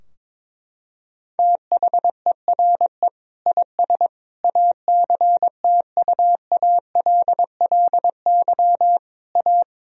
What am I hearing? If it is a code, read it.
Morse code, heard as a keyed tone, about 22 wpm, 700 Hz. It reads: THERE IS ACTUALLY A
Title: Sentences from Top 1000 Words - Review Part 4 - 22wpm